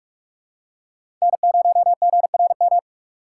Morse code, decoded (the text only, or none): N0GRM